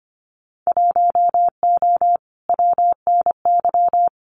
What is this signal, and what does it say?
Morse code, heard as a keyed tone, about 25 wpm, 700 Hz. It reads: 1O WNY
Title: Winter Field Day Exchanges Encore 25wpm